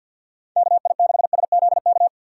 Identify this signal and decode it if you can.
Morse code, heard as a keyed tone, about 45 wpm, 700 Hz. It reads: KI6SZK
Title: USA Call Signs - 6 Characters - 4x Character Spacing - 50wpm